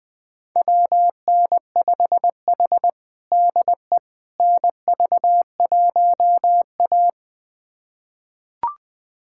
Morse code, heard as a keyed tone, about 20 wpm, 700 Hz. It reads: WN5H DE NV1A
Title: USA Callsign DE Callsign - 4 Characters - 20wpm